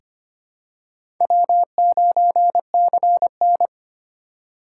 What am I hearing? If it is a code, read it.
Morse code, heard as a keyed tone, about 25 wpm, 700 Hz. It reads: W9CN
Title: USA Call Signs - 2x Character Spacing - 25wpm